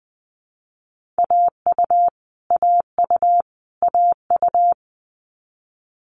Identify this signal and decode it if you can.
Morse code, heard as a keyed tone, about 20 wpm, 700 Hz. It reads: AU AU AU
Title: Multi-letter Phonemes Repeated 3x 20wpm